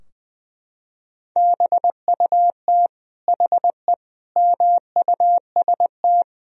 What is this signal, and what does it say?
Morse code, heard as a keyed tone, about 20 wpm, 700 Hz. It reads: BUT HE MUST
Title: Sentences from Top 1000 Words - Review Part 3 - 20wpm